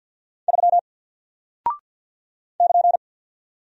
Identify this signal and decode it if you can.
Morse code, heard as a keyed tone, about 48 wpm, 700 Hz. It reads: 3 <KN>
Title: Single Character (Letter:Number:Punctuation:Prosign) 50wpm